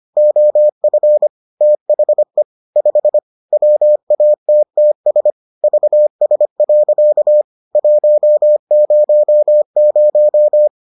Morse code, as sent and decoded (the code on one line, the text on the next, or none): --- ..-. / - .... / ..... / .-- .- - - ... / ...- ... .-.-.- / .---- ----- -----
OF THE 5 WATTS VS. 100